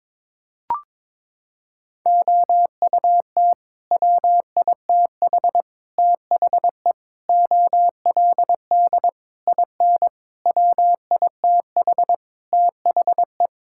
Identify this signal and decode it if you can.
Morse code, heard as a keyed tone, about 22 wpm, 700 Hz. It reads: OUT WITH THE OLD IN WITH THE
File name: practice / 22wpm / Sets of 8 Words - 22wpm